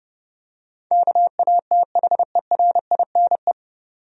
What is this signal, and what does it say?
Morse code, heard as a keyed tone, about 31 wpm, 700 Hz. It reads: KATHERINE
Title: USA Common Names - 30wpm